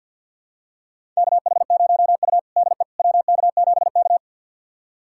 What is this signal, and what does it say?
Morse code, heard as a keyed tone, about 52 wpm, 700 Hz. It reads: KH0U DE WK7K
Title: USA Callsign DE Callsign - 4 Characters - 50wpm